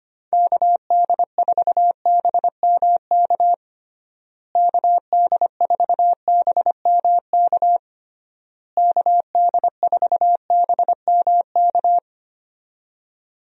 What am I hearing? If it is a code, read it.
Morse code, heard as a keyed tone, about 25 wpm, 700 Hz. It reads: KD4BMK KD4BMK KD4BMK
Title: USA Call Signs - 6 Characters - Repeated 3x - 25wpm